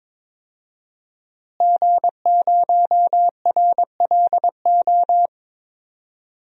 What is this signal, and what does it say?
Morse code, heard as a keyed tone, about 22 wpm, 700 Hz. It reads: G0RLO
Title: DXCC-100 Call Signs - 5 Characters - Repeated 3x - 22wpm